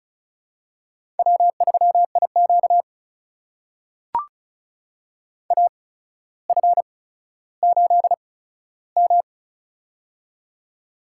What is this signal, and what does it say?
Morse code, heard as a keyed tone, about 34 wpm, spaced as for 15 wpm, 700 Hz. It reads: W3IQ AF8M
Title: USA Call Signs - 8x Character Spacing - 35wpm